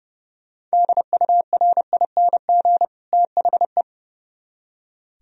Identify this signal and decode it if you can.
Morse code, heard as a keyed tone, about 37 wpm, 700 Hz. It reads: DURING THE
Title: Speed-Racing - Sets of 2 Words 30wpm